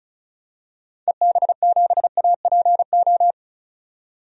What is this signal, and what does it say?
Morse code, heard as a keyed tone, about 36 wpm, 700 Hz. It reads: EB7APO
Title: DXCC-100 Call Signs - 6 Characters - 35wpm